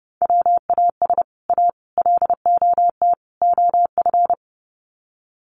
Morse code, heard as a keyed tone, about 30 wpm, 700 Hz. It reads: WAS A LOT OF
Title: Sets of 4 Words 30wpm